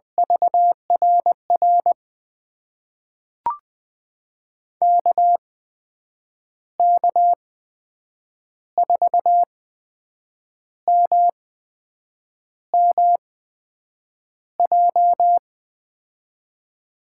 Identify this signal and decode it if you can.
Morse code, heard as a keyed tone, about 20 wpm, spaced as for 7 wpm, 700 Hz. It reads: VRR KK4MMJ
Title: USA Call Signs - 6 Characters - 8x Character Spacing - 20wpm